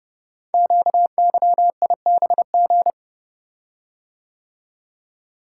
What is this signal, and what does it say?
Morse code, heard as a keyed tone, about 30 wpm, 700 Hz. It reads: QYIBG
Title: Code Groups - 5 Letters - 30wpm